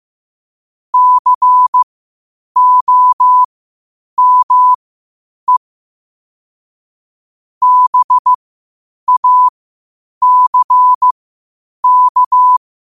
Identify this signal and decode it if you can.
Morse code, heard as a keyed tone, about 15 wpm, spaced as for 8 wpm, 1000 Hz. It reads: COME BACK
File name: qotdfiles / 8wpm